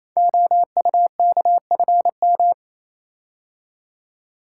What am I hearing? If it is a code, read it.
Morse code, heard as a keyed tone, about 29 wpm, 700 Hz. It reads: OUKFM
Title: Code Groups - 5 Letters - 28wpm